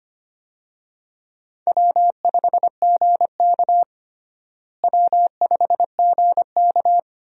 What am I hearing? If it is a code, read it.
Morse code, heard as a keyed tone, about 25 wpm, 700 Hz. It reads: W5GK W5GK
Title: USA Call Signs - 4 Characters - Repeated 3x - 25wpm